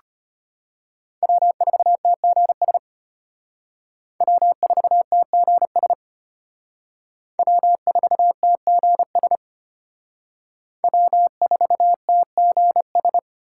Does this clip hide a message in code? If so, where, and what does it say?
Yes: Morse code from 1.2 to 13.2 s, heard as a keyed tone, about 31 wpm, 700 Hz. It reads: W4TGS W4TGS W4TGS W4TGS